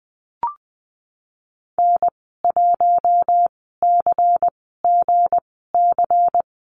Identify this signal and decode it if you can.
Morse code, heard as a keyed tone, about 20 wpm, spaced as for 15 wpm, 700 Hz. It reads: N1CGC
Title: 173 Lesson 28g - Number 9 - Call Signs 20wpm